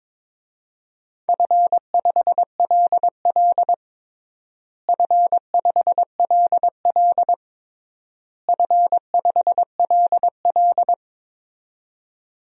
Morse code, heard as a keyed tone, about 22 wpm, 700 Hz. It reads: F5LL F5LL F5LL
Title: DXCC-100 Call Signs - 4 Characters - Repeated 3x - 22wpm